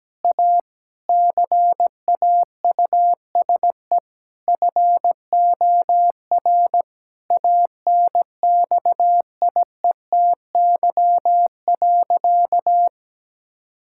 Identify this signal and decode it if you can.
Morse code, heard as a keyed tone, about 17 wpm, 700 Hz. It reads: A CAUSE FOR ANXIETY.
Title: Sentences from Top 1600 Words 17wpm